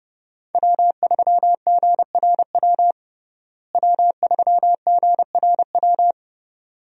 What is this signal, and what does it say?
Morse code, heard as a keyed tone, about 30 wpm, 700 Hz. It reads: W3GRW W3GRW
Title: USA Call Signs - 5 Characters - Repeated 2x - 30wpm